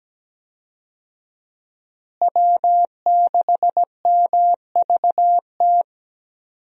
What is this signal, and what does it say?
Morse code, heard as a keyed tone, about 17 wpm, 700 Hz. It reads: W6MVT
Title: USA Call Signs - 5 Characters - Repeated 2x - 17wpm